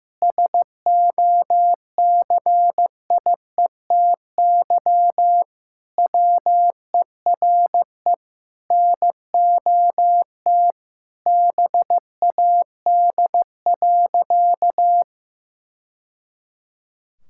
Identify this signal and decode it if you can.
Morse code, heard as a keyed tone, about 15 wpm, 700 Hz. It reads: SOCIETY WERE NOT BAD.
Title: Sentences from Top 1000 Words - Review Part 5 - 15wpm